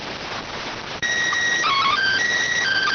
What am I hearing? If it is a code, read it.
SSTV (slow-scan television), PD 290.